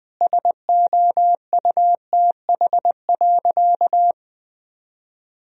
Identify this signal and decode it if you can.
Morse code, heard as a keyed tone, about 21 wpm, 700 Hz. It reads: SOUTH.
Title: Sentences from Top 1000 Words - Review Part 2 - 20wpm